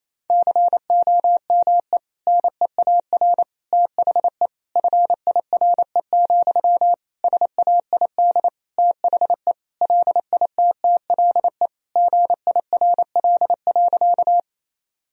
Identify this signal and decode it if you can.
Morse code, heard as a keyed tone, about 28 wpm, 700 Hz. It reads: COME NEAR THE FIRE, SAID THE LITTLE GIRL.
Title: Sentences from Top 1000 Words - Review Part 1 - 28wpm